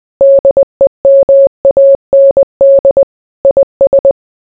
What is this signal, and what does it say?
Morse code, heard as a keyed tone, about 20 wpm, 550 Hz. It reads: DEMAND IS